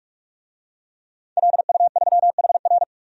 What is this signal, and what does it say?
Morse code, heard as a keyed tone, about 46 wpm, 700 Hz. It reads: LU3HR
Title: DXCC-100 Call Signs - 5 Characters - 45wpm